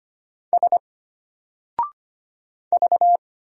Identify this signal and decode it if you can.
Morse code, heard as a keyed tone, about 25 wpm, 700 Hz. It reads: S V
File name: Single-Letter-Number-25wpm